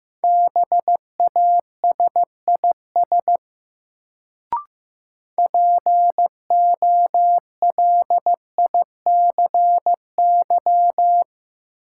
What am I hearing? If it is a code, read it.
Morse code, heard as a keyed tone, about 16 wpm, 700 Hz. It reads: BASIS POLICY
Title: Sentences from Top 1000 Words - Review Part 5 - New Word Drill - 15wpm